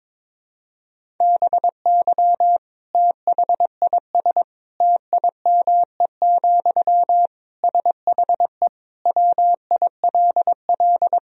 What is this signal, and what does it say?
Morse code, heard as a keyed tone, about 22 wpm, 700 Hz. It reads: BY THIS TIME, SHE WILL
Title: Sentences from Top 1000 Words - Review Part 4 - 22wpm